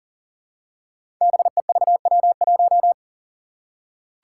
Morse code, heard as a keyed tone, about 42 wpm, 700 Hz. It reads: BEVW1
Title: Five Challenging Letter-Number Combinations 40wpm